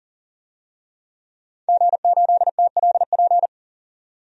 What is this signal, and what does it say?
Morse code, heard as a keyed tone, about 40 wpm, 700 Hz. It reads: G8TLP
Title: DXCC-100 Call Signs - 5 Characters - 40wpm